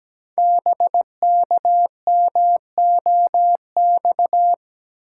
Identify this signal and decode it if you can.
Morse code, heard as a keyed tone, about 17 wpm, 700 Hz. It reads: BKMOX
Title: Code Groups - 5 Letters - 17wpm